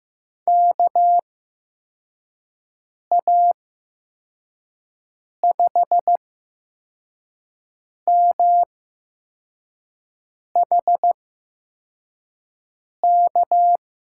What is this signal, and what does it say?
Morse code, heard as a keyed tone, about 15 wpm, spaced as for 4 wpm, 700 Hz. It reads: KA5MHK
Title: USA Call Signs - 6 Characters - 8x Character Spacing - 15wpm